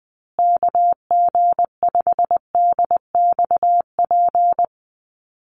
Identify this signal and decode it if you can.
Morse code, heard as a keyed tone, about 19 wpm, 700 Hz. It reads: KG5DXP
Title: USA Call Signs Repeated 3x 20wpm